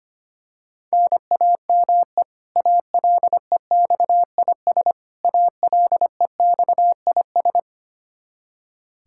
Morse code, heard as a keyed tone, about 25 wpm, 700 Hz. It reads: NAME ALEXIS ALEXIS
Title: K1USN SST QSO Practice 25wpm